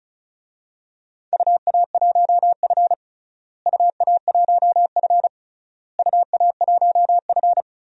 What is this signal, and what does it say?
Morse code, heard as a keyed tone, about 36 wpm, 690 Hz. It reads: UA1F UA1F UA1F
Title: DXCC-100 Call Signs - 4 Characters - Repeated 3x - 35wpm